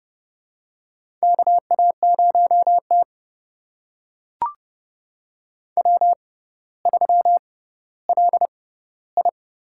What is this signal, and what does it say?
Morse code, heard as a keyed tone, about 31 wpm, spaced as for 16 wpm, 700 Hz. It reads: KA0T W3LI
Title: USA Call Signs - 4 Characters - 6x Character Spacing - 30wpm